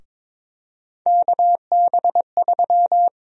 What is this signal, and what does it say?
Morse code, heard as a keyed tone, about 22 wpm, 700 Hz. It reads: KB3
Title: USA Call Sign Prefixes - Repeated 2x - 22wpm